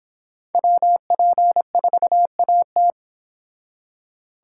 Morse code, heard as a keyed tone, about 28 wpm, 700 Hz. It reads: WP4AT